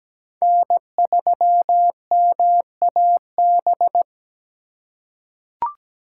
Morse code, heard as a keyed tone, about 17 wpm, 700 Hz. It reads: N3MAB
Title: USA Call Signs - 5 Characters - 3x Character Spacing - 17wpm